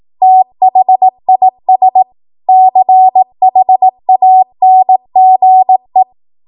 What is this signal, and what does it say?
Morse code, heard as a keyed tone, about 18 wpm, 750 Hz. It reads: THIS CHANGE
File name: Morse / 18WPM